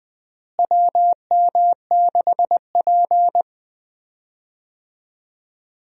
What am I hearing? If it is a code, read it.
Morse code, heard as a keyed tone, about 20 wpm, 700 Hz. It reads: WM6P
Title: USA Call Signs - 4 Characters - Repeated 3x - 20wpm